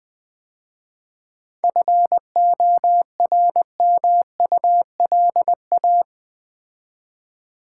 Morse code, heard as a keyed tone, about 20 wpm, 690 Hz. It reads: FORMULA
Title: 7 Letter Words - First 3 Letters Given - 20wpm